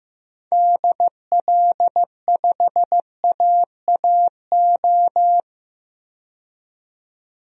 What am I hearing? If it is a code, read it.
Morse code, heard as a keyed tone, about 15 wpm, 700 Hz. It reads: DL5AAO
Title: DXCC-100 Call Signs - 6 Characters - 15wpm